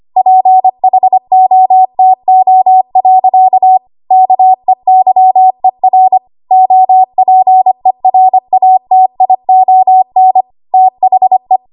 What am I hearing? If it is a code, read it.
Morse code, heard as a keyed tone, about 25 wpm, 750 Hz. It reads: PHOTO. KEYER OPERATION THE
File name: Archive / 25 WPM / 25WPM